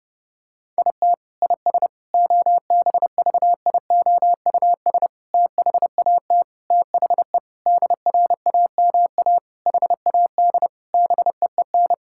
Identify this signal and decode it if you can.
Morse code, heard as a keyed tone, about 30 wpm, 700 Hz. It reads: IT IS OBVIOUS THAT THE DRAMA HAD BEEN
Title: Sentences from Top 1600 Words 30wpm